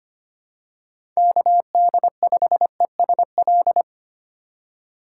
Morse code, heard as a keyed tone, about 25 wpm, 700 Hz. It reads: KD5ESL
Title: USA Call Signs - 6 Characters - 3x Character Spacing - 25wpm